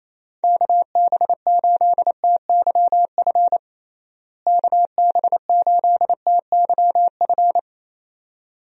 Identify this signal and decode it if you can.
Morse code, heard as a keyed tone, about 27 wpm, 700 Hz. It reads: KB8TYF KB8TYF
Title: USA Call Signs - 6 Characters - Repeated 2x - 28wpm